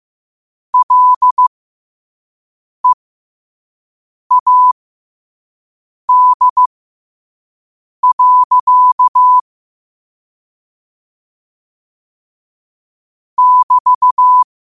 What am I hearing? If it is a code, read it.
Morse code, heard as a keyed tone, about 15 wpm, spaced as for 5 wpm, 1000 Hz. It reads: LEAD. <BT>
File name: qotdfiles / 5wpm